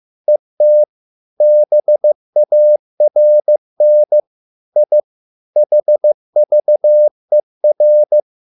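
Morse code, heard as a keyed tone, about 15 wpm, 600 Hz. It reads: ET BARN I HVER
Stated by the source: Morse code at a slow 15 wpm from CW med Gnister.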